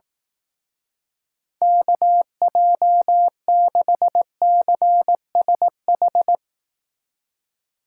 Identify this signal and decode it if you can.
Morse code, heard as a keyed tone, about 18 wpm, 700 Hz. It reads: KJ6CSH
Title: Speed Racing - USA 2x3 Call Signs 15wpm